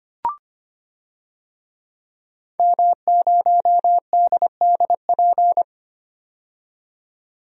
Morse code, heard as a keyed tone, about 25 wpm, 700 Hz. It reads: M0DDP
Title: DXCC-100 Call Signs - 5 Characters - 25wpm